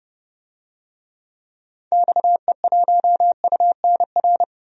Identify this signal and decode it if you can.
Morse code, heard as a keyed tone, about 30 wpm, 700 Hz. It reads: XE1UNR
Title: DXCC-100 Call Signs - 6 Characters - 30wpm